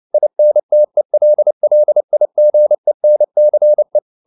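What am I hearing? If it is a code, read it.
Morse code, heard as a keyed tone, about 29 wpm, 600 Hz. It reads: INTELLIGENCE
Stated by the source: Morse code at 29 wpm, sending Ken Follett's novel Never in English, from CW med Gnister.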